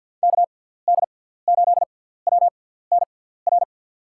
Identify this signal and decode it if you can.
Morse code, heard as a keyed tone, about 53 wpm, spaced as for 17 wpm, 700 Hz. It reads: KD8WNR
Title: USA Call Signs - 6 Characters - 6x Character Spacing - 50wpm